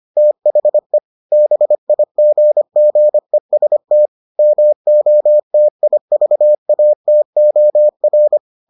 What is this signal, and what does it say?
Morse code, heard as a keyed tone, about 25 wpm, 600 Hz. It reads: THE BIGGEST MOTIVATOR